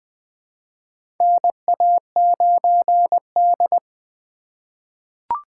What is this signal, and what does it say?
Morse code, heard as a keyed tone, about 20 wpm, 700 Hz. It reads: NA9D
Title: USA Call Signs - 4 Characters - 3x Character Spacing - 20wpm